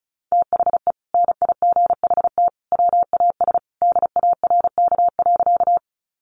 Morse code, heard as a keyed tone, about 34 wpm, 700 Hz. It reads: THE NIGHT WAS DARK.
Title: Sentences from Top 500 Words 35wpm